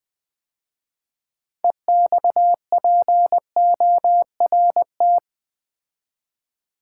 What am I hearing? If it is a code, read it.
Morse code, heard as a keyed tone, about 20 wpm, 700 Hz. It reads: EXPORT
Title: 6 Letter Words - First 3 Letters Given - 20wpm